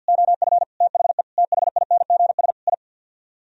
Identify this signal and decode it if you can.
Morse code, heard as a keyed tone, about 64 wpm, 700 Hz. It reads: OF THE THINGS I